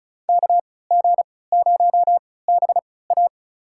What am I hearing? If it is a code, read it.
Morse code, heard as a keyed tone, about 34 wpm, spaced as for 20 wpm, 700 Hz. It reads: KG0BA
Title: USA Call Signs - 5 Characters - 3x Character Spacing - 35wpm